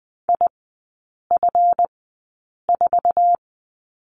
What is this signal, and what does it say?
Morse code, heard as a keyed tone, about 20 wpm, spaced as for 8 wpm, 700 Hz. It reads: IF4
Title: 152 Lesson 25e - Letter p - Three Characters 20wpm